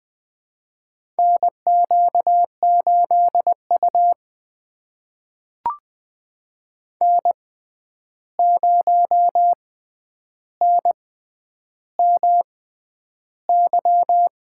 Morse code, heard as a keyed tone, about 20 wpm, spaced as for 10 wpm, 700 Hz. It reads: NQ8U N0NMY
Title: USA Call Signs - 6x Character Spacing - 20wpm